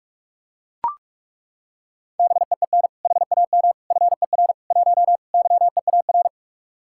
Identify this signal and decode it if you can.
Morse code, heard as a keyed tone, about 37 wpm, 700 Hz. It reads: BEEN HAM FER 1 YEAR
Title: Speed-Racing Common QSO Phrases Encore - 30wpm